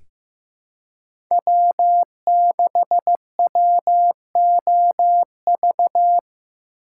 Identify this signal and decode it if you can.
Morse code, heard as a keyed tone, about 15 wpm, 700 Hz. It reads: W6WOV
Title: USA Call Signs - 5 Characters - 4x Character Spacing - 15wpm